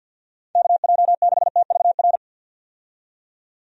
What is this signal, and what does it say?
Morse code, heard as a keyed tone, about 51 wpm, 700 Hz. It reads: KJ6TVR
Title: USA Call Signs - 6 Characters - 50wpm